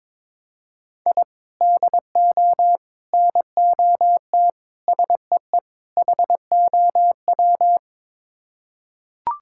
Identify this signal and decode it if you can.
Morse code, heard as a keyed tone, about 22 wpm, 700 Hz. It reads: I DO NOT SEE HOW
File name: practice / 22wpm / Build A Phrase - 5-Word Phrases - 22wpm